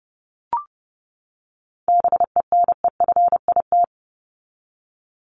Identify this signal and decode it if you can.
Morse code, heard as a keyed tone, about 30 wpm, 700 Hz. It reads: BENEFIT
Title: Top 751-1000 Words 30wpm